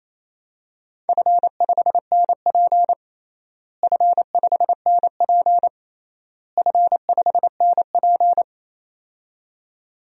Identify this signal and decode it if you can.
Morse code, heard as a keyed tone, about 27 wpm, 700 Hz. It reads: F5NP F5NP F5NP
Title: DXCC-100 Call Signs - 4 Characters - Repeated 3x - 28wpm